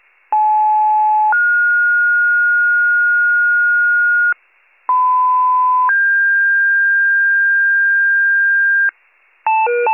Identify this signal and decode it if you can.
Two-tone page, heard as a one-second tone, then a three-second tone, about 860 then 1430 Hz, about 1010 then 1600 Hz, with noise. It is heard from 0.3 to 8.9 s.